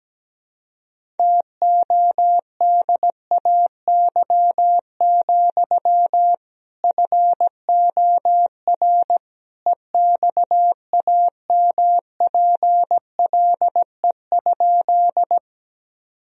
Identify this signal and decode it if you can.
Morse code, heard as a keyed tone, about 17 wpm, 700 Hz. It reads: TODAY, FOR EXAMPLE?
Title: Sentences from Top 1000 Words - Review Part 1 - 17wpm